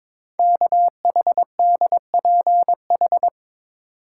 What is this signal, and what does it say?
Morse code, heard as a keyed tone, about 22 wpm, 700 Hz. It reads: KHDPH
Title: Code Groups - 5 Letters - 22wpm